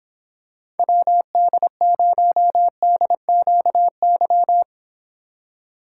Morse code, heard as a keyed tone, about 26 wpm, 700 Hz. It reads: WD0DQY